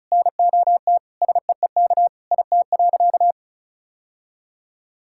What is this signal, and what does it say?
Morse code, heard as a keyed tone, about 36 wpm, 700 Hz. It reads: NOT SEEK IT.
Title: Sentences from Top 1600 Words 35wpm